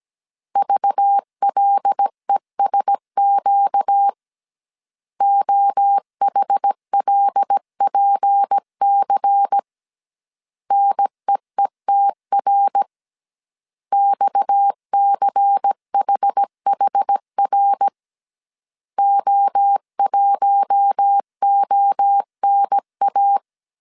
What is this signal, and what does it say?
Morse code, heard as a keyed tone, about 17 wpm, 800 Hz. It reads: VLESQ OHLPC NEETR XCHHR O1ONA
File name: CW / DET / 1 6 1500C S A 16wpm 2635